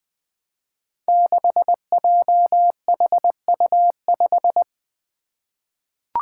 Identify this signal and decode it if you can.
Morse code, heard as a keyed tone, about 20 wpm, 700 Hz. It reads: 6JHU5
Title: Five Challenging Letter-Number Combinations 20wpm